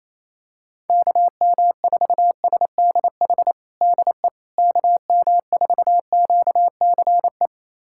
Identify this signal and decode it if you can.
Morse code, heard as a keyed tone, about 28 wpm, 700 Hz. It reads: KM4SDH DE KM4QCE
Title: USA Callsign DE Callsign 28wpm